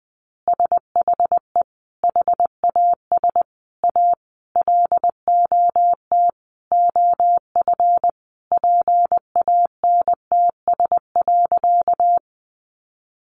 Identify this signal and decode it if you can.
Morse code, heard as a keyed tone, about 20 wpm, 700 Hz. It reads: SHE HAS A LOT OF PANTS.
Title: Additional Easy Phrases 20wpm